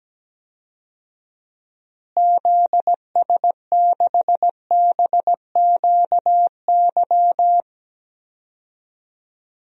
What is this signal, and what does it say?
Morse code, heard as a keyed tone, about 17 wpm, 700 Hz. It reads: ZS6BQY
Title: DXCC-100 Call Signs - 6 Characters - Repeated 3x - 17wpm